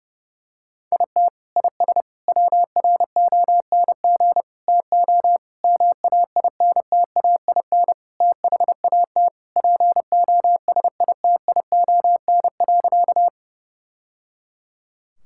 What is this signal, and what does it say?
Morse code, heard as a keyed tone, about 30 wpm, 700 Hz. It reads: IT IS WRONG TO MAINTAIN THAT POSITION.
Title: Sentences from Top 1000 Words - Review Part 5 - 30wpm